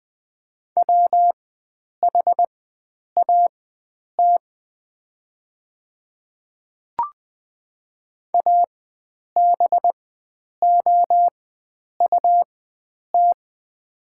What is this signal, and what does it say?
Morse code, heard as a keyed tone, about 20 wpm, spaced as for 9 wpm, 700 Hz. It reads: WHAT ABOUT
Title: 4x Character Spacing - Top 100 Words - 20wpm